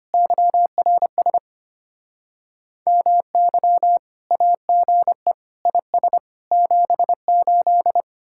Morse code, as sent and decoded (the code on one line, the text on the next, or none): -.-- .-. ... / -- -.-- / .- --. / .. ... / --... ---..
YRS MY AGE IS 78